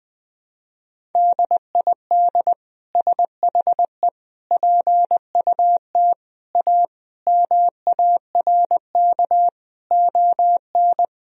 Morse code, heard as a keyed tone, about 20 wpm, 700 Hz. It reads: DID SHE PUT A MARK ON
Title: Sentences from Top 1000 Words - Review Part 1 - 20wpm